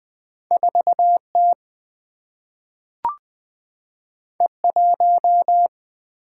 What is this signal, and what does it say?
Morse code, heard as a keyed tone, about 20 wpm, 700 Hz. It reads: 4T E1
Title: Two Letter-Number Combinations - 20wpm